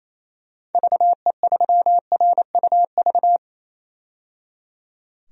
Morse code, heard as a keyed tone, about 28 wpm, 700 Hz. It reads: VE3RUV
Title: DXCC-100 Call Signs - 6 Characters - 28wpm